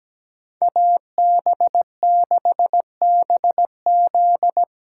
Morse code, heard as a keyed tone, about 19 wpm, 700 Hz. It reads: AB6BZ
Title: Speed Racing - USA 2x2 Call Signs 17wpm